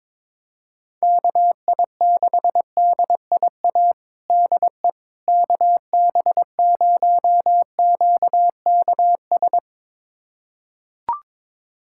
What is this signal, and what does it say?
Morse code, heard as a keyed tone, about 22 wpm, 700 Hz. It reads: KI6DIA DE KB0QKS
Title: USA Callsign DE Callsign 22wpm